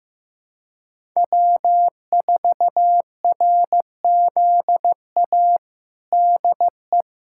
Morse code, heard as a keyed tone, about 14 wpm, 700 Hz. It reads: W4RZA DE
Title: Practice QSO - Exchange 4 - Ending - 15wpm